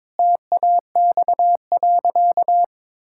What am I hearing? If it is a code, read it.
Morse code, heard as a keyed tone, about 22 wpm, 700 Hz. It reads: TAX.
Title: Sentences from Top 1000 Words - Review Part 5 - 22wpm